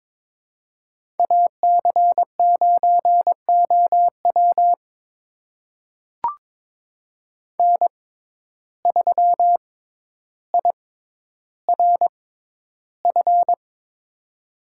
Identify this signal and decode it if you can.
Morse code, heard as a keyed tone, about 22 wpm, spaced as for 11 wpm, 700 Hz. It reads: AC9OW N3IRF
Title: USA Call Signs - 6x Character Spacing - 22wpm